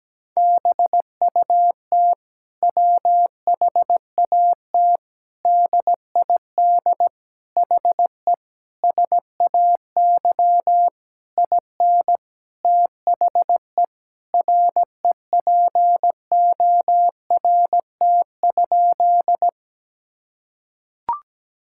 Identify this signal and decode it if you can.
Morse code, heard as a keyed tone, about 17 wpm, 700 Hz. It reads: BUT WHAT DID HE SAY IN THE REPORT?
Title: Sentences from Top 1000 Words - Review Part 6 - 17wpm